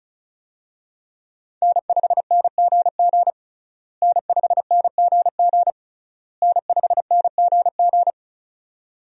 Morse code, heard as a keyed tone, about 36 wpm, 700 Hz. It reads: N5NGG N5NGG N5NGG
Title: USA Call Signs - 5 Characters - Repeated 3x - 35wpm